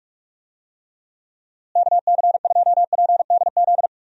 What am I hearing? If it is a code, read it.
Morse code, heard as a keyed tone, about 46 wpm, 700 Hz. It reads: KC2PDZ
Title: USA Call Signs - 6 Characters - Repeated 3x - 45wpm